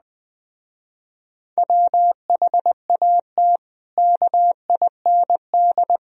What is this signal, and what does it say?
Morse code, heard as a keyed tone, about 20 wpm, 700 Hz. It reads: WHAT KIND